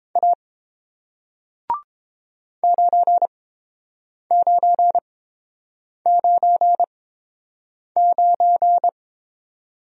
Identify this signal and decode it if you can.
Morse code, heard as a keyed tone, about 30 wpm, spaced as for 7 wpm, 700 Hz. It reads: A 9999